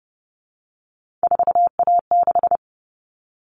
Morse code, heard as a keyed tone, about 30 wpm, 700 Hz. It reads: 4A6
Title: All Cut Numbers - 3-Digit 30wpm